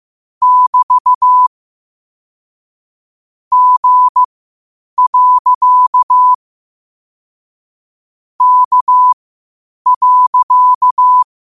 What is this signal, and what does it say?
Morse code, heard as a keyed tone, about 15 wpm, spaced as for 8 wpm, 1000 Hz. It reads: <BT> G. K.
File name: qotdfiles / 8wpm